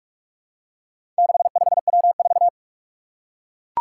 Morse code, heard as a keyed tone, about 44 wpm, 700 Hz. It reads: 65W4
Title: Four Challenging Letter-Number Combinations 45wpm